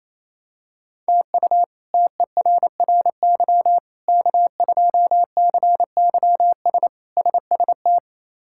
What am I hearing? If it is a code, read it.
Morse code, heard as a keyed tone, about 28 wpm, 700 Hz. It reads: TU TERRY K2CYS SST
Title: K1USN SST QSO Practice - 28wpm